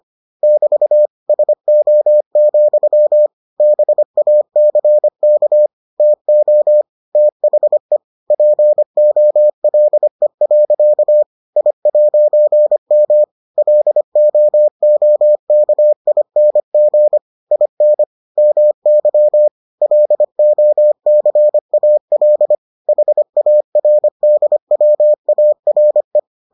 Morse code, heard as a keyed tone, about 25 wpm, 600 Hz. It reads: <BT> SO, BACK TO THE POLE. I'M LOOKING IN MY LOCAL HARDWARE